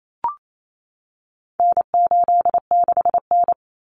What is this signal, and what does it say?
Morse code, heard as a keyed tone, about 28 wpm, 700 Hz. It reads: N86N
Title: Common Cut Numbers - 4-Digit 28wpm